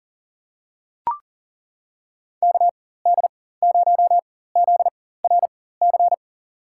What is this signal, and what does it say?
Morse code, heard as a keyed tone, about 40 wpm, spaced as for 19 wpm, 700 Hz. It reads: KD0ZRC
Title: USA Call Signs - 6 Characters - 4x Character Spacing - 40wpm